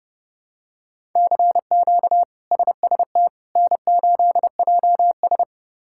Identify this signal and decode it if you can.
Morse code, heard as a keyed tone, about 30 wpm, 700 Hz. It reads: CQ SST N8JS
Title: K1USN SST QSO Practice - 30wpm